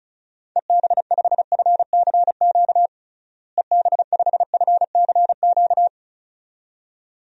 Morse code, heard as a keyed tone, about 35 wpm, 700 Hz. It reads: EB5FCQ EB5FCQ
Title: DXCC-100 Call Signs - 6 Characters - Repeated 2x - 35wpm